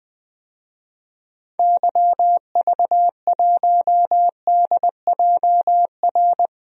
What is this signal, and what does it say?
Morse code, heard as a keyed tone, about 21 wpm, 700 Hz. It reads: YV1DJR
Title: DXCC-100 Call Signs - 6 Characters - 20wpm